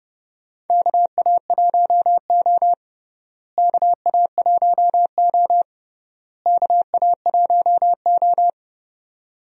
Morse code, heard as a keyed tone, about 30 wpm, 700 Hz. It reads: KA1O KA1O KA1O
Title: USA Call Signs - 4 Characters - Repeated 3x - 30wpm